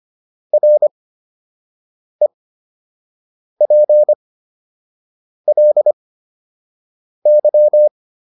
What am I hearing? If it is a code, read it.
Morse code, heard as a keyed tone, about 25 wpm, spaced as for 6 wpm, 600 Hz. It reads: REPLY